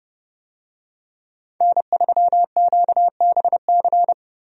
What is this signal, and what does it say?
Morse code, heard as a keyed tone, about 29 wpm, 700 Hz. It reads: N3QBC
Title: USA Call Signs - 5 Characters - Repeated 2x - 30wpm